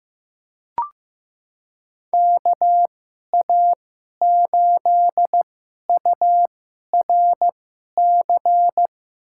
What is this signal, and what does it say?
Morse code, heard as a keyed tone, about 15 wpm, spaced as for 11 wpm, 700 Hz. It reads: KA8URC
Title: USA Call Signs - 6 Characters - 2x Character Spacing - 15wpm